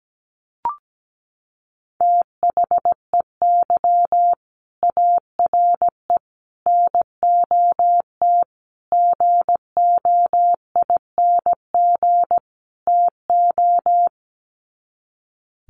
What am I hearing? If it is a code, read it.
Morse code, heard as a keyed tone, about 17 wpm, 700 Hz. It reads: THEY ARE NOT GOING TO
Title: Sets of 5 Words Encore2 17wpm